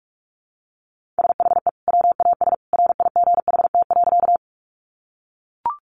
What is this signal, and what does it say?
Morse code, heard as a keyed tone, about 44 wpm, 700 Hz. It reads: SHE WAS RIGHT.
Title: Sentences from Top 200 Words 45wpm